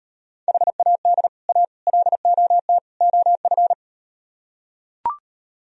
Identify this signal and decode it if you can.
Morse code, heard as a keyed tone, about 38 wpm, 700 Hz. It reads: HAD A LOT OF